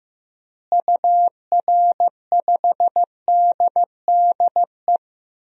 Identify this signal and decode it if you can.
Morse code, heard as a keyed tone, about 15 wpm, 700 Hz. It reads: UR5DDE
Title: DXCC-100 Call Signs - 6 Characters - Repeated 3x - 15wpm